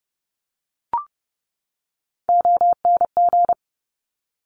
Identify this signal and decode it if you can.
Morse code, heard as a keyed tone, about 30 wpm, 690 Hz. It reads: ONG
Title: Top Three Letter Combinations in Words 30wpm